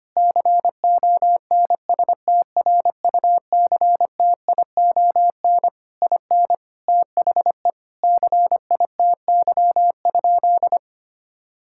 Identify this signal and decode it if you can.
Morse code, heard as a keyed tone, about 25 wpm, 700 Hz. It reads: CONSTRUCTION IN THE CITY?
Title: Sentences from Top 1600 Words 25wpm